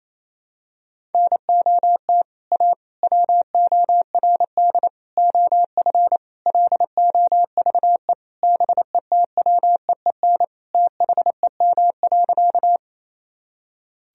Morse code, heard as a keyed tone, about 28 wpm, 700 Hz. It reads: NOT A WORD OF LOVE BETWEEN THEM.
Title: Sentences from Top 1000 Words - Review Part 1 - 28wpm